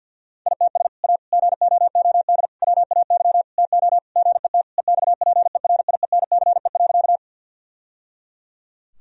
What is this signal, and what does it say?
Morse code, heard as a keyed tone, about 50 wpm, 700 Hz. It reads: ITS A GOOD WAY TO GET EXPERIENCE.
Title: Sentences from Top 1000 Words - Review Part 3 - 50wpm